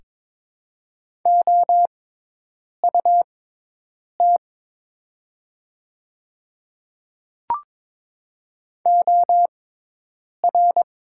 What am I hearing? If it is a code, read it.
Morse code, heard as a keyed tone, about 22 wpm, spaced as for 8 wpm, 700 Hz. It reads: OUT OR
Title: 6x Character Spacing - Top 100 Words - 22wpm